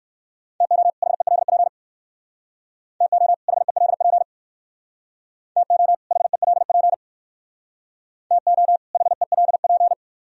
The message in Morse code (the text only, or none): TO HELP TO HELP TO HELP TO HELP